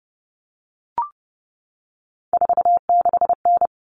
Morse code, heard as a keyed tone, about 30 wpm, 700 Hz. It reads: 46N
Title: Common Cut Numbers - 3-Digit 30wpm